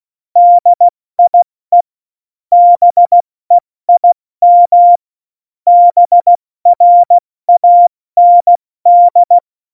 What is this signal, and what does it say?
Morse code, heard as a keyed tone, about 16 wpm, spaced as for 14 wpm, 700 Hz. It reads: DIE BEIM BRAND
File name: Morsekurs / Witze / Witz 27. Februar 2018, 70 F80